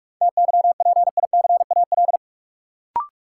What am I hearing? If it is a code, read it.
Morse code, heard as a keyed tone, about 45 wpm, 700 Hz. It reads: TYPICAL
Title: Sentences from Top 1100 Words - New Word Drill Alt - 45wpm